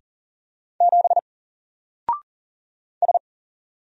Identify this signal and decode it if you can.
Morse code, heard as a keyed tone, about 40 wpm, 700 Hz. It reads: 7 S
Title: Single Character (Letter:Number:Punctuation:Prosign) 40wpm